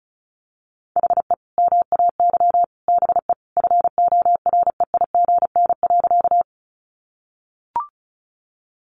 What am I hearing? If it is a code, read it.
Morse code, heard as a keyed tone, about 34 wpm, 700 Hz. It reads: HE MAY BE FOREIGN.
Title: Sentences from Top 1400 Words 35wpm